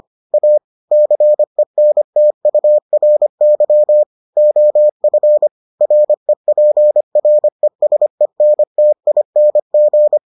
Morse code, heard as a keyed tone, about 25 wpm, 600 Hz. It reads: A CENTURY OF REPRESENTING